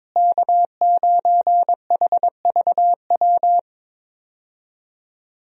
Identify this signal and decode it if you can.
Morse code, heard as a keyed tone, about 22 wpm, 700 Hz. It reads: K9HVW
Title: USA Call Signs - 5 Characters - Repeated 2x - 22wpm